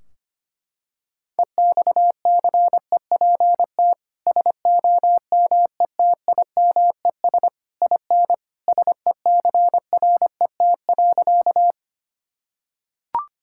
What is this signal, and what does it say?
Morse code, heard as a keyed tone, about 25 wpm, 700 Hz. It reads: EXCEPT SOMETIMES IN SECRET.
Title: Sentences from Top 1600 Words 25wpm